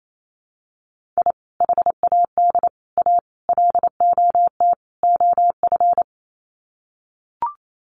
Morse code, heard as a keyed tone, about 28 wpm, 700 Hz. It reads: I HAD A LOT OF
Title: Sets of 5 Words Encore 28wpm